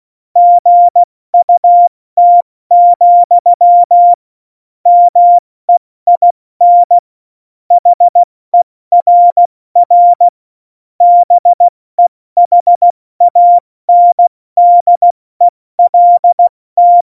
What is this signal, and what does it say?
Morse code, heard as a keyed tone, about 16 wpm, 700 Hz. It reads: GUT, MEIN HERR BEHANDELT
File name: Morsekurs / Witze / Witz 06. Oktober 2020, 70 F80